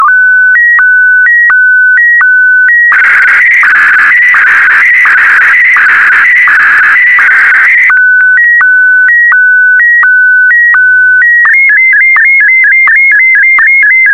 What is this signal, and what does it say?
SSTV (slow-scan television), PD 90.